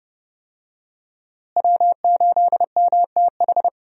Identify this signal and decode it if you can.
Morse code, heard as a keyed tone, about 30 wpm, 700 Hz. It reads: W8MTH
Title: USA Call Signs - 5 Characters - Repeated 2x - 30wpm